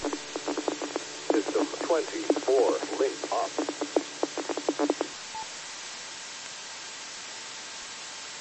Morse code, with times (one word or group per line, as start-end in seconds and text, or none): none